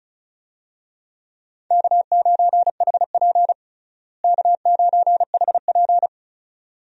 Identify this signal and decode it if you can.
Morse code, heard as a keyed tone, about 34 wpm, 700 Hz. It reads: K9HP K9HP
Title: USA Call Signs - 4 Characters - Repeated 2x - 35wpm